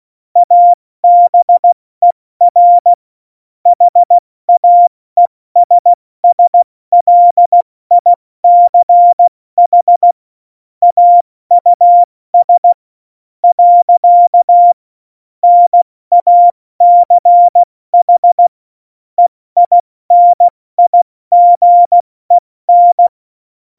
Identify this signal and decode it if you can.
Morse code, heard as a keyed tone, about 16 wpm, 700 Hz. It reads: ABER HAESSLICH AUS . NACH EINIGEN
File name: Morsekurs / Witze / Witz 23. Oktober 2018, 70 F80